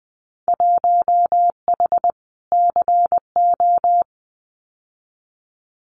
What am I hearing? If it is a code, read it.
Morse code, heard as a keyed tone, about 20 wpm, 700 Hz. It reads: 1H CO
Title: Winter Field Day Exchanges 20wpm